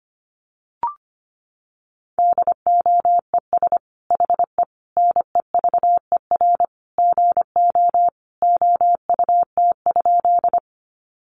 Morse code, heard as a keyed tone, about 25 wpm, 700 Hz. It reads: DOES HE NEVER GO OUT?
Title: Sentences from Top 300 Words 25wpm